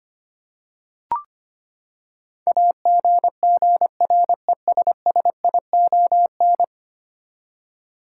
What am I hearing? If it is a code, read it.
Morse code, heard as a keyed tone, about 25 wpm, 700 Hz. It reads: AGGRESSION
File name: Ten Letter Words - 25wpm